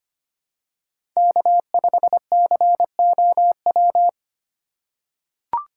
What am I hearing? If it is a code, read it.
Morse code, heard as a keyed tone, about 25 wpm, 700 Hz. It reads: K5COW
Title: USA Call Signs - 5 Characters - 3x Character Spacing - 25wpm